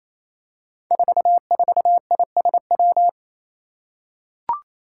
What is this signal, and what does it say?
Morse code, heard as a keyed tone, about 28 wpm, 700 Hz. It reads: 44ISW
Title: Five Challenging Letter-Number Combinations 28wpm